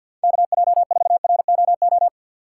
Morse code, heard as a keyed tone, about 49 wpm, 700 Hz. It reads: KJ4ROO
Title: USA Call Signs - 6 Characters - 6x Character Spacing - 50wpm